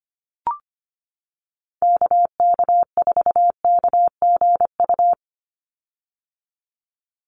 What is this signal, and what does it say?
Morse code, heard as a keyed tone, about 25 wpm, 700 Hz. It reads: KK4KGU
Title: USA Call Signs 25wpm